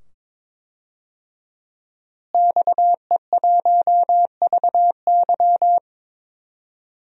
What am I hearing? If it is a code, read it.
Morse code, heard as a keyed tone, about 22 wpm, 700 Hz. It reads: XE1VY
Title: DXCC-100 Call Signs - 5 Characters - Repeated 2x - 22wpm